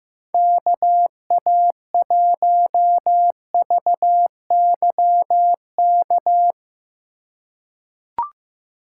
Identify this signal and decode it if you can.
Morse code, heard as a keyed tone, about 15 wpm, 700 Hz. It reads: KA1VY K
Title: USA Call Signs - 3x Character Spacing - 15wpm